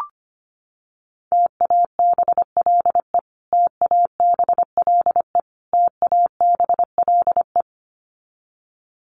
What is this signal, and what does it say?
Morse code, heard as a keyed tone, about 25 wpm, 700 Hz. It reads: TABLE TABLE TABLE
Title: Top 500 Words Repeated 25wpm